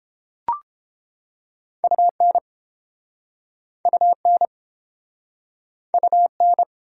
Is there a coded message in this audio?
Yes: Morse code from 1.8 to 6.6 s, heard as a keyed tone, about 34 wpm, 700 Hz. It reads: UN UN UN